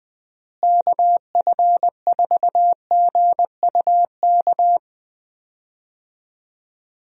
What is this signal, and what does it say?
Morse code, heard as a keyed tone, about 20 wpm, 700 Hz. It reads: KF4GUK
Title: USA Call Signs - 6 Characters - Repeated 2x - 20wpm